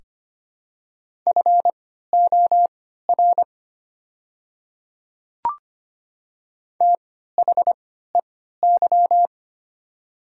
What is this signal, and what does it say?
Morse code, heard as a keyed tone, about 25 wpm, spaced as for 14 wpm, 690 Hz. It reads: FOR THEY
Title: 3x Character Spacing - Top 100 Words - 25wpm